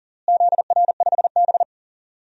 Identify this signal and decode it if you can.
Morse code, heard as a keyed tone, about 40 wpm, 700 Hz. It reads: ZR5B
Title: DXCC-100 Call Signs - 4 Characters - Repeated 3x - 40wpm